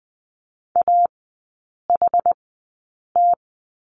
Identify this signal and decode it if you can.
Morse code, heard as a keyed tone, about 19 wpm, spaced as for 8 wpm, 690 Hz. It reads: AHT
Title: 060 Lesson 11e - Letter h - Three Characters 20wpm